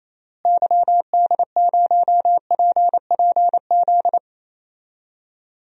Morse code, heard as a keyed tone, about 28 wpm, 700 Hz. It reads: YD0PPZ
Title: DXCC-100 Call Signs - 6 Characters - Repeated 3x - 28wpm